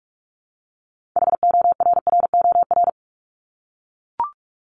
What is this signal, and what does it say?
Morse code, heard as a keyed tone, about 44 wpm, 700 Hz. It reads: HORROR
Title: Top 1501-1750 Words 45wpm